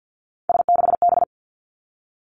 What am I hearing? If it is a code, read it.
Morse code, heard as a keyed tone, about 50 wpm, 700 Hz. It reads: S6B
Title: Three Challenging Letter-Number Combinations 50wpm